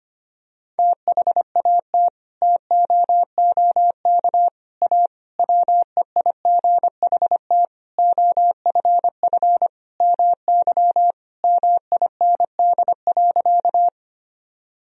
Morse code, heard as a keyed tone, about 25 wpm, 700 Hz. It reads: THAT TOOK A WEIGHT OFF MY MIND.
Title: Sentences from Top 1000 Words - Review Part 3 - 25wpm